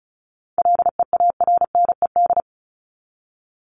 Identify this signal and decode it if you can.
Morse code, heard as a keyed tone, about 34 wpm, 700 Hz. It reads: LEARNED
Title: ED Suffix - Top 100 Words with Base Words 35wpm